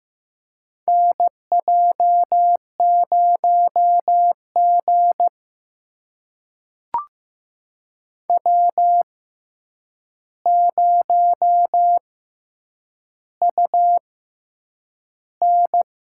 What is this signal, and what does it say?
Morse code, heard as a keyed tone, about 15 wpm, spaced as for 8 wpm, 700 Hz. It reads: NJ0G W0UN
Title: USA Call Signs - 4 Characters - 6x Character Spacing - 15wpm